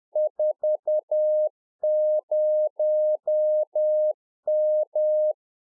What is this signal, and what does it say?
Morse code, heard as a keyed tone, about 10 wpm, 610 Hz. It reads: 40M